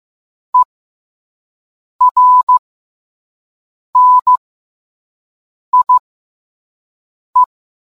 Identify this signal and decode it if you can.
Morse code, heard as a keyed tone, about 15 wpm, spaced as for 5 wpm, 1000 Hz. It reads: ERNIE